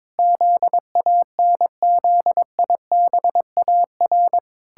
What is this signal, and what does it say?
Morse code, heard as a keyed tone, about 22 wpm, 700 Hz. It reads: ZANZIBAR
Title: Pangram Challenge v2 - 22wpm